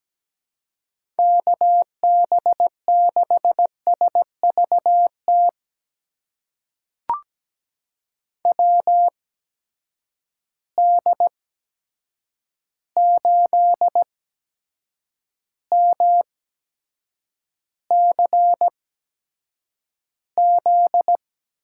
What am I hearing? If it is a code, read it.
Morse code, heard as a keyed tone, about 17 wpm, spaced as for 7 wpm, 700 Hz. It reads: KB6SVT WD8MCZ
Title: USA Call Signs - 6 Characters - 8x Character Spacing - 17wpm